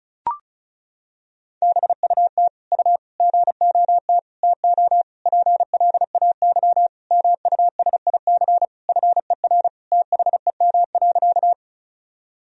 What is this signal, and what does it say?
Morse code, heard as a keyed tone, about 36 wpm, 700 Hz. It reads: BUT U GOT TO PLAY MUSIC FER THEM.
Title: Sentences from Top 300 Words and Abbreviations 35wpm